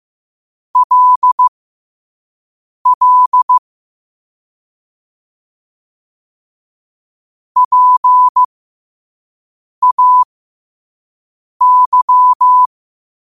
Morse code, heard as a keyed tone, about 15 wpm, spaced as for 5 wpm, 1000 Hz. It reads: LL PAY